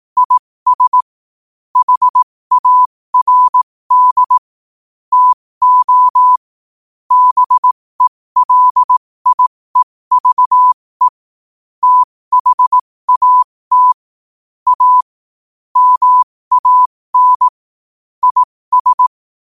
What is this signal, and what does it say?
Morse code, heard as a keyed tone, about 18 wpm, spaced as for 15 wpm, 1000 Hz. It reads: IS HARD TO BELIEVE THAT A MAN IS
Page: New quotes every day in morse code at 15 Words per minute.